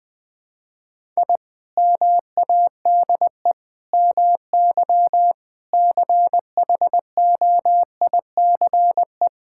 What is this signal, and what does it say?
Morse code, heard as a keyed tone, about 20 wpm, 700 Hz. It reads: I MADE MY CHOICE
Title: Sentences from Top 1000 Words - Review Part 6 - 20wpm